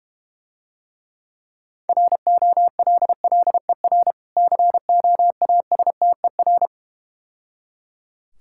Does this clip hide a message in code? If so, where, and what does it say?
Yes: Morse code from 1.9 to 6.7 s, heard as a keyed tone, about 32 wpm, 700 Hz. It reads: ROLLER COASTER